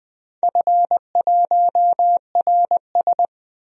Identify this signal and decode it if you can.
Morse code, heard as a keyed tone, about 20 wpm, 700 Hz. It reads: F1RS
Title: DXCC-100 Call Signs - 4 Characters - 20wpm